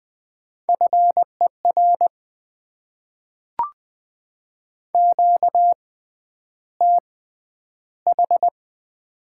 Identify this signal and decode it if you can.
Morse code, heard as a keyed tone, about 20 wpm, spaced as for 10 wpm, 690 Hz. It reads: FER QTH
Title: Top 20 QSO Elements - 6x Character Spacing - 20wpm